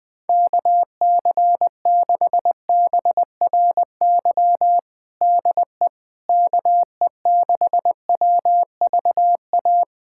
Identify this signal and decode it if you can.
Morse code, heard as a keyed tone, about 20 wpm, 700 Hz. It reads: KC6BRY DE KE6WVA
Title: USA Callsign DE Callsign - 6 Characters - 20wpm